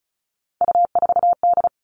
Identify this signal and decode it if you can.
Morse code, heard as a keyed tone, about 34 wpm, 700 Hz. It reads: U4D
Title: All Cut Numbers - 3-Digit 35wpm